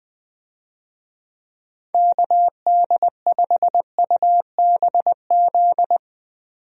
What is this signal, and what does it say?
Morse code, heard as a keyed tone, about 20 wpm, 700 Hz. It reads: KD5UBZ
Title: USA Call Signs - 6 Characters - 20wpm